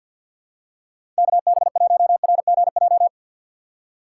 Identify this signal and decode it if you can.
Morse code, heard as a keyed tone, about 49 wpm, 700 Hz. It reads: KB1RGJ
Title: USA Call Signs - 6 Characters - 4x Character Spacing - 50wpm